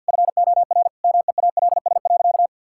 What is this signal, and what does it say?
Morse code, heard as a keyed tone, about 50 wpm, 700 Hz. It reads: FOR MEALS.
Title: Pangram Challenge v2 - 50wpm